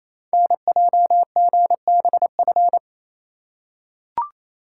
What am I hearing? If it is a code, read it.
Morse code, heard as a keyed tone, about 27 wpm, 700 Hz. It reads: NJGBF
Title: Code Groups - 5 Letters - 28wpm